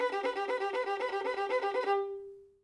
There is no code.